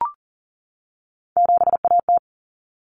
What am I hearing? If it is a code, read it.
Morse code, heard as a keyed tone, about 40 wpm, 700 Hz. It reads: 7AT
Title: Common Cut Numbers - 3-Digit 40wpm